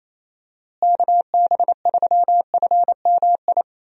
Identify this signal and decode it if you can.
Morse code, heard as a keyed tone, about 28 wpm, 700 Hz. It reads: KB3FMI
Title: USA Call Signs - 6 Characters - Repeated 2x - 28wpm